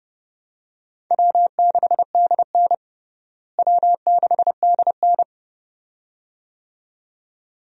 Morse code, heard as a keyed tone, about 30 wpm, 700 Hz. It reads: W6DN W6DN
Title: USA Call Signs - 4 Characters - Repeated 2x - 30wpm